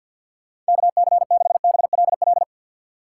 Morse code, heard as a keyed tone, about 64 wpm, 700 Hz. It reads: KC6BLL